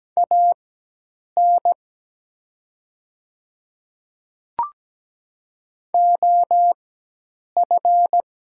Morse code, heard as a keyed tone, about 17 wpm, spaced as for 8 wpm, 700 Hz. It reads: AN OF
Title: 4x Character Spacing - Top 300 Words - 17wpm